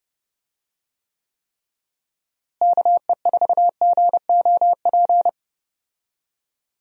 Morse code, heard as a keyed tone, about 30 wpm, 700 Hz. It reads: KE4GOP
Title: USA Call Signs - 6 Characters - Repeated 3x - 30wpm